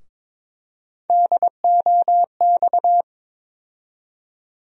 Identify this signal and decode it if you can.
Morse code, heard as a keyed tone, about 22 wpm, 700 Hz. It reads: DOX
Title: USA Call Sign Postfixes - 3 Letters 22wpm